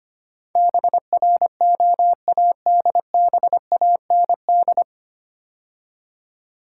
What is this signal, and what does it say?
Morse code, heard as a keyed tone, about 25 wpm, 700 Hz. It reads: BROADBAND
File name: practice / 25wpm / 9 Letter Words - First 3 Letters Given - 25wpm